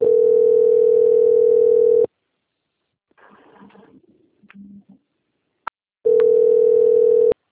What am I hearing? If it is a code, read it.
Ringback tone (North America), 440 + 480 Hz.